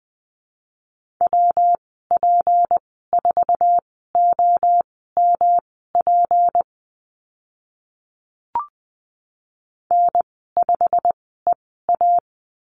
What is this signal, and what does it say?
Morse code, heard as a keyed tone, about 20 wpm, spaced as for 14 wpm, 700 Hz. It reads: WP4OMP N5EA
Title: 173 Lesson 28g - Number 9 - Call Signs 20wpm